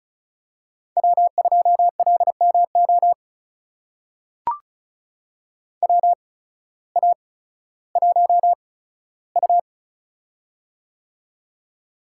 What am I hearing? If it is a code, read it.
Morse code, heard as a keyed tone, about 36 wpm, spaced as for 16 wpm, 700 Hz. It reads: W2LMO WA1U
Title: USA Call Signs - 8x Character Spacing - 35wpm